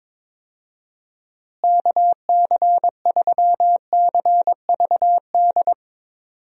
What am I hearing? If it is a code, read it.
Morse code, heard as a keyed tone, about 22 wpm, 700 Hz. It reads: KC3CVD
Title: USA Call Signs - 6 Characters - Repeated 3x - 22wpm